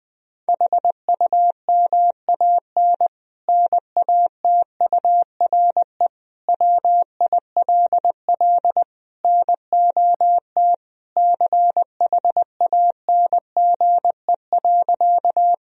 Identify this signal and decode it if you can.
Morse code, heard as a keyed tone, about 20 wpm, 700 Hz. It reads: HUMAN NATURE WILL NOT CHANGE.
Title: Sentences from Top 1000 Words - Review Part 3 - 20wpm